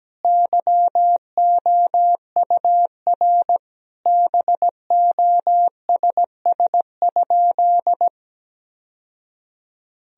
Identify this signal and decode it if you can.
Morse code, heard as a keyed tone, about 17 wpm, 700 Hz. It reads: YOUR BOSS?
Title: Sentences from Top 1000 Words - Review Part 4 - 17wpm